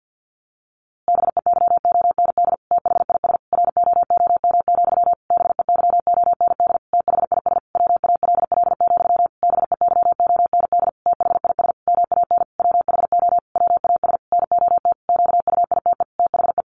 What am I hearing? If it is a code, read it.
Morse code, heard as a keyed tone, about 50 wpm, 700 Hz. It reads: BEYOND THIS ROOM, BEYOND THIS WALL, BEYOND THIS MAN WHO WAS NOT QUITE THE
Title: A Sound Of Thunder 50wpm